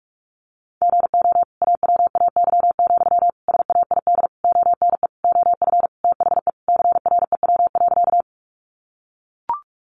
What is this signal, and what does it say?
Morse code, heard as a keyed tone, about 44 wpm, 700 Hz. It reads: GO AWAY, SAID ONE OF THE CREW.
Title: Sentences from Top 1500 Words 45wpm